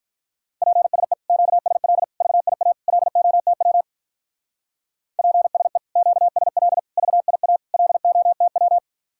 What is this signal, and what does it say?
Morse code, heard as a keyed tone, about 44 wpm, 700 Hz. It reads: PSE QSL VIA LOTW PSE QSL VIA LOTW
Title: Speed-Racing Common QSO Phrases Encore - 35wpm